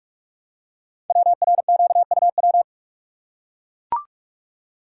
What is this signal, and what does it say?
Morse code, heard as a keyed tone, about 44 wpm, 700 Hz. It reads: WRQUW
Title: Code Groups - 5 Letters - 45wpm